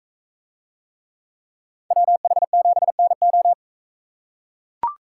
Morse code, heard as a keyed tone, about 44 wpm, 700 Hz. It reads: WH7NO